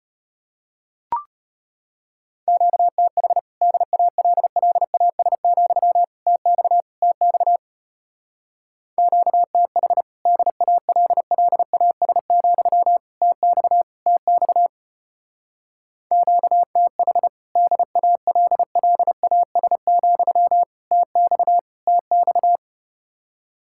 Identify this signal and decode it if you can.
Morse code, heard as a keyed tone, about 33 wpm, 700 Hz. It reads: QTH DALLAS, TX TX QTH DALLAS, TX TX QTH DALLAS, TX TX